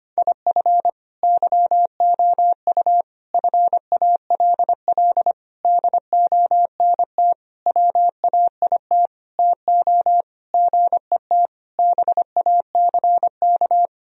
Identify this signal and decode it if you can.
Morse code, heard as a keyed tone, about 25 wpm, 700 Hz. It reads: IF YOU FALL DONT WAIT TO GET BACK
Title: Sentences from Top 1000 Words - Review Part 2 - 25wpm